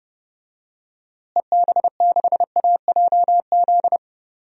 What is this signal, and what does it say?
Morse code, heard as a keyed tone, about 30 wpm, 700 Hz. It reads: EB6AJZ
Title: DXCC-100 Call Signs - 6 Characters - Repeated 3x - 30wpm